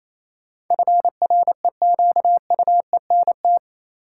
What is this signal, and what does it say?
Morse code, heard as a keyed tone, about 28 wpm, 700 Hz. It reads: FREQUENT
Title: Sentences from Top 1400 Words - New Word Drill Alt - 28wpm